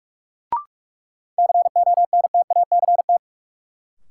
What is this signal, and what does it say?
Morse code, heard as a keyed tone, about 44 wpm, 700 Hz. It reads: CONTACT
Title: Sentences from Top 1000 Words - Review Part 7 - New Word Drill - 45wpm